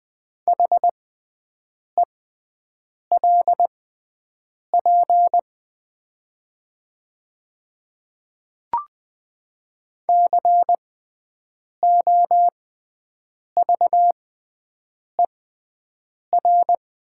Morse code, heard as a keyed tone, about 20 wpm, spaced as for 7 wpm, 700 Hz. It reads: HELP COVER
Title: 6x Character Spacing - Top 500 Words - 20wpm